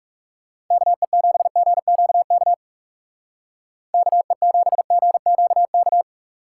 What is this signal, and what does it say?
Morse code, heard as a keyed tone, about 35 wpm, 700 Hz. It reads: KE7GQK KE7GQK
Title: Speed Racing - USA Call Signs 30wpm